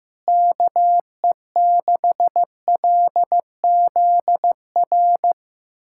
Morse code, heard as a keyed tone, about 16 wpm, 700 Hz. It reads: KE6LZR
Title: USA Callsign DE Callsign - 6 Characters - 15wpm